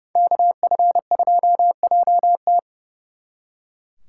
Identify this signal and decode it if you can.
Morse code, heard as a keyed tone, about 36 wpm, 700 Hz. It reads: KF2JT
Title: Speed Racing - USA 2x2 Call Signs 30wpm